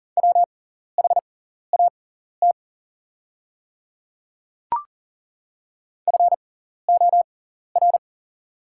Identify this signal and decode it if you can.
Morse code, heard as a keyed tone, about 40 wpm, spaced as for 14 wpm, 700 Hz. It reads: WHAT FOR
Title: 6x Character Spacing - Top 100 Words - 40wpm